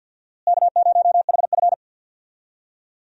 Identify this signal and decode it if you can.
Morse code, heard as a keyed tone, about 46 wpm, 700 Hz. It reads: K0HF
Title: USA Call Signs - 4 Characters - Repeated 2x - 50wpm